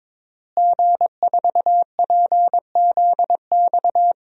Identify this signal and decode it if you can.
Morse code, heard as a keyed tone, about 22 wpm, 700 Hz. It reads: G4PZX
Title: DXCC-100 Call Signs - 5 Characters - Repeated 3x - 22wpm